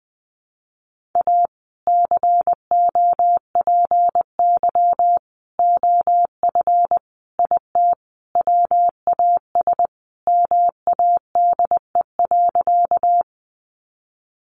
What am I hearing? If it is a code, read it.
Morse code, heard as a keyed tone, about 19 wpm, 700 Hz. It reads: A COPY OF IT WAS MADE.
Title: Sentences from Top 1400 Words 20wpm